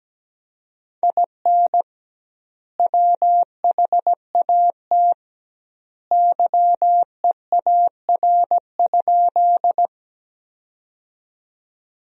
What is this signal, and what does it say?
Morse code, heard as a keyed tone, about 17 wpm, 700 Hz. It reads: IN WHAT YEAR?
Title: 2x Word-Spacing - Sentences from Top 200 Words - 17wpm